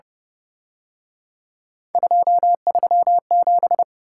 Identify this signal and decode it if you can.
Morse code, heard as a keyed tone, about 30 wpm, 700 Hz. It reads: 237